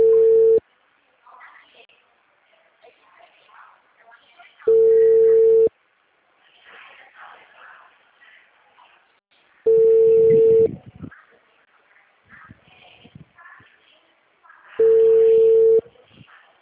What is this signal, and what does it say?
Ringback tone (China, 450 Hz), with speech.